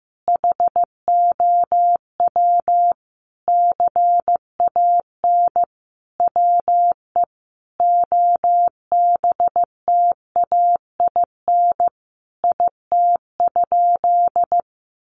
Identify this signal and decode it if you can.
Morse code, heard as a keyed tone, about 15 wpm, 700 Hz. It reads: HOW CAN WE OBTAIN IT?
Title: Sentences from Top 1300 Words 15wpm